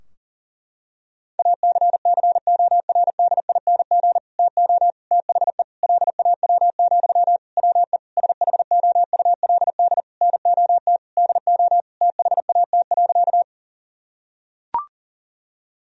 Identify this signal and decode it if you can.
Morse code, heard as a keyed tone, about 37 wpm, 700 Hz. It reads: ACCORDING TO THE LAW, WE SHOULD NOT DO THAT.
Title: Sentences from Top 1000 Words - Review Part 4 - 40wpm